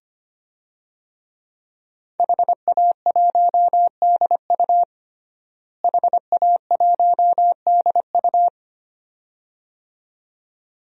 Morse code, heard as a keyed tone, about 25 wpm, 700 Hz. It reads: HA1DU HA1DU
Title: DXCC-100 Call Signs - 5 Characters - Repeated 2x - 25wpm